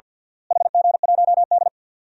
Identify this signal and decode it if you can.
Morse code, heard as a keyed tone, about 47 wpm, 700 Hz. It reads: HG1D
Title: DXCC-100 Call Signs - 4 Characters - Repeated 2x - 50wpm